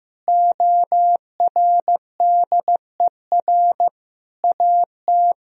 Morse code, heard as a keyed tone, about 16 wpm, 700 Hz. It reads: ORDER AT
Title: Sentences from Top 1000 Words - Review Part 2 - 15wpm